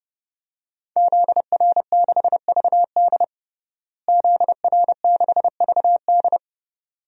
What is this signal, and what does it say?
Morse code, heard as a keyed tone, about 30 wpm, 700 Hz. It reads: ZR6VD ZR6VD
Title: DXCC-100 Call Signs - 5 Characters - Repeated 2x - 30wpm